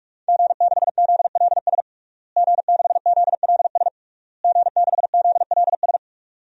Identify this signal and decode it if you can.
Morse code, heard as a keyed tone, about 46 wpm, 700 Hz. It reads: G6ZLS G6ZLS G6ZLS
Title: DXCC-100 Call Signs - 5 Characters - Repeated 3x - 45wpm